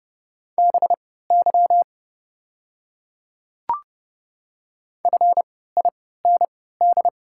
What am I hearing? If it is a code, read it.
Morse code, heard as a keyed tone, about 29 wpm, spaced as for 17 wpm, 700 Hz. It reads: BY FIND
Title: 3x Character Spacing - Top 300 Words - 30wpm